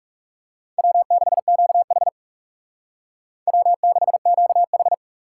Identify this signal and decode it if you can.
Morse code, heard as a keyed tone, about 38 wpm, 700 Hz. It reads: W6QH W6QH